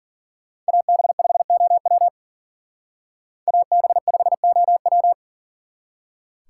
Morse code, heard as a keyed tone, about 52 wpm, 700 Hz. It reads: AB5OW AB5OW